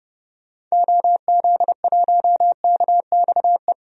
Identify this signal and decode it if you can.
Morse code, heard as a keyed tone, about 30 wpm, 700 Hz. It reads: OZ1KXE
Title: DXCC-100 Call Signs - 6 Characters - 30wpm